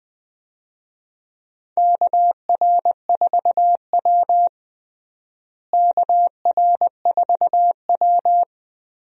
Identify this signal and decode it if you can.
Morse code, heard as a keyed tone, about 20 wpm, 700 Hz. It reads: KR4W KR4W
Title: USA Call Signs - 4 Characters - Repeated 3x - 20wpm